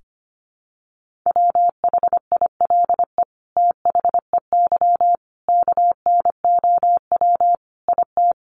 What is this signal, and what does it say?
Morse code, heard as a keyed tone, about 25 wpm, 700 Hz. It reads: WHILE THEY KNOW IT
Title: Sentences from Top 1400 Words 25wpm